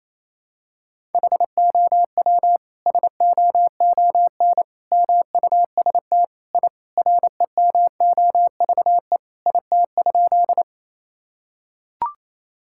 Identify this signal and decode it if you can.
Morse code, heard as a keyed tone, about 27 wpm, 700 Hz. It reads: HOW SOON MUST I REMOVE IT?
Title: Sentences from Top 1000 Words - Review Part 5 - 28wpm